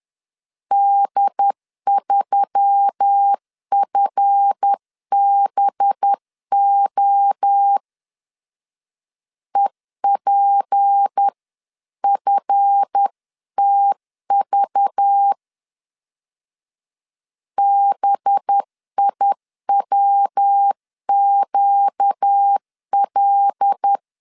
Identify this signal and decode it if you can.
Morse code, heard as a keyed tone, about 10 wpm, 800 Hz. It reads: D3FBO EPFTV BIWQL
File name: LEZIONI / 7 GF9Z / 7 GF 10wpm 2514